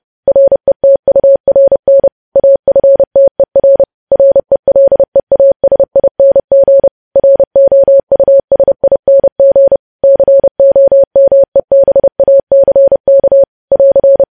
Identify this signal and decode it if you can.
Morse code, heard as a keyed tone, about 30 wpm, 550 Hz. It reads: RETURN AFTER RELEASING ROUSING COMEBACK <AR>
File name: podcast / News headlines-1x-30x30